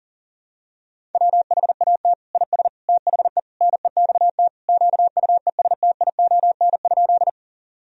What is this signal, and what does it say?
Morse code, heard as a keyed tone, about 40 wpm, 700 Hz. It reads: WHAT IS THE NEXT QUESTION?
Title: Sentences from Top 1000 Words - Review Part 1 - 40wpm